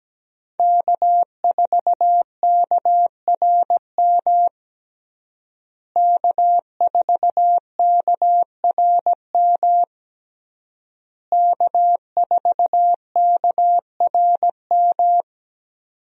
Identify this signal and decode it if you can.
Morse code, heard as a keyed tone, about 17 wpm, 700 Hz. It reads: K4KRM K4KRM K4KRM
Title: USA Call Signs - 5 Characters - Repeated 3x - 17wpm